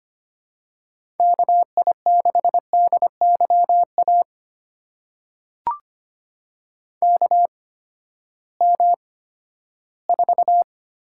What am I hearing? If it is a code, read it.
Morse code, heard as a keyed tone, about 25 wpm, spaced as for 14 wpm, 700 Hz. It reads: KI6DYA KM4
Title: USA Call Signs - 6 Characters - 8x Character Spacing - 25wpm